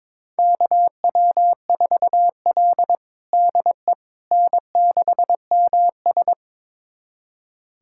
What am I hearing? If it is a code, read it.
Morse code, heard as a keyed tone, about 21 wpm, 700 Hz. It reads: KW4L DE N6MS
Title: USA Callsign DE Callsign - 4 Characters - 22wpm